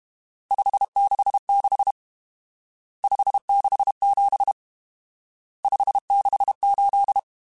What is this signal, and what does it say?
Morse code, heard as a keyed tone, about 32 wpm, 800 Hz. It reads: 566 567 568
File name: RSTs / RST-mp3s / Various RST @30